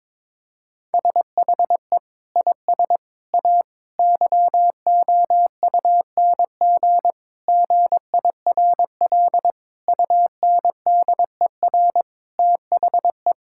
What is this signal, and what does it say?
Morse code, heard as a keyed tone, about 22 wpm, 700 Hz. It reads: SHE IS A YOUNG GIRL UNDER THE
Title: Sentences from Top 1600 Words 22wpm